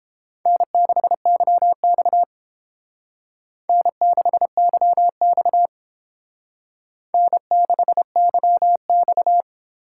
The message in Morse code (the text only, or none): N6YX N6YX N6YX